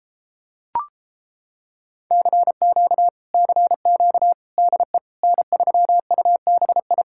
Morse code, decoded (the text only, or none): CQ CQ DE N3UBI